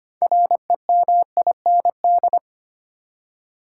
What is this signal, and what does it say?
Morse code, heard as a keyed tone, about 25 wpm, 700 Hz. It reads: REMIND
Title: Six Letter Words - 25wpm